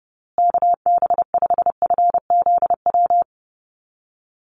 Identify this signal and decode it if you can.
Morse code, heard as a keyed tone, about 30 wpm, 700 Hz. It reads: KB5FZW
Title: USA Call Signs Repeated 3x 30wpm